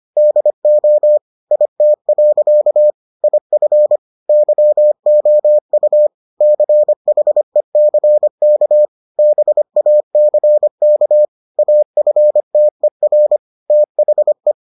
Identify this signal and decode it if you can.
Morse code, heard as a keyed tone, about 25 wpm, 600 Hz. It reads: DO IT. IF YOU CHECK BACK AFTER THE